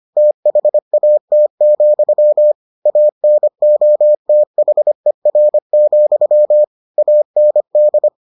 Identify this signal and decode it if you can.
Morse code, heard as a keyed tone, about 25 wpm, 600 Hz. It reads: THAT, ANOTHER, AND